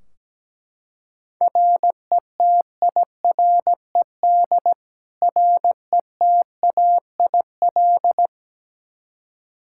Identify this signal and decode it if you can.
Morse code, heard as a keyed tone, about 17 wpm, 700 Hz. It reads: RETIRED RETAIL
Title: K1USN SST QSO Practice 17wpm